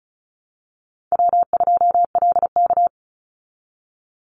Morse code, heard as a keyed tone, about 34 wpm, 700 Hz. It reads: W2LK
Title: Call Signs Repeated 3x - 35wpm